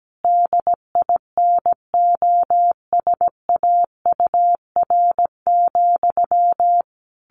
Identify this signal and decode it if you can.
Morse code, heard as a keyed tone, about 17 wpm, 700 Hz. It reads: DINOSAUR,
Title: A Sound Of Thunder 17wpm